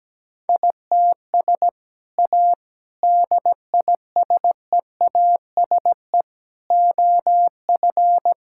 Morse code, heard as a keyed tone, about 17 wpm, 700 Hz. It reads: ITS A DISEASE OF
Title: Sentences from Top 1000 Words - Review Part 6 - 17wpm